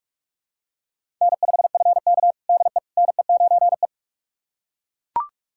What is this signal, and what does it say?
Morse code, heard as a keyed tone, about 41 wpm, 690 Hz. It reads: N5FK DE NE9E
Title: USA Callsign DE Callsign - 4 Characters - 45wpm